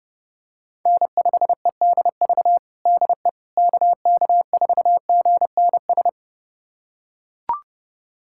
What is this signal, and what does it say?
Morse code, heard as a keyed tone, about 30 wpm, 700 Hz. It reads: N5EDV DE KK4GNS
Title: USA Callsign DE Callsign 30wpm